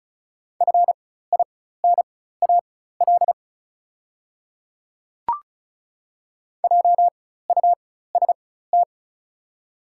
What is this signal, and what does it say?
Morse code, heard as a keyed tone, about 34 wpm, spaced as for 16 wpm, 690 Hz. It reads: FINAL JUST
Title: 4x Character Spacing - Top 500 Words - 35wpm